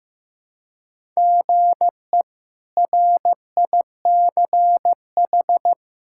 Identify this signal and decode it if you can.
Morse code, heard as a keyed tone, about 15 wpm, 700 Hz. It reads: GE RICH
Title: K1USN SST QSO Practice - 15wpm